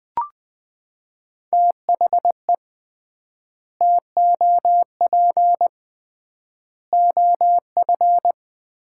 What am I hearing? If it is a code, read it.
Morse code, heard as a keyed tone, about 20 wpm, 700 Hz. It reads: THE TOP OF
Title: 3x Word-Spacing - Sets of 3 Words - 20wpm